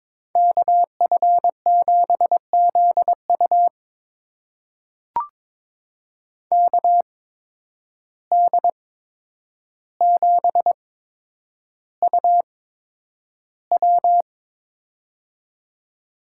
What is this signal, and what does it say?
Morse code, heard as a keyed tone, about 22 wpm, spaced as for 9 wpm, 700 Hz. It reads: KF7ZU KD7UW
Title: USA Call Signs - 5 Characters - 8x Character Spacing - 22wpm